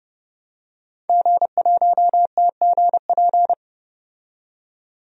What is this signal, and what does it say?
Morse code, heard as a keyed tone, about 30 wpm, 700 Hz. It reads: G1TGP
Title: DXCC-100 Call Signs - 5 Characters - 30wpm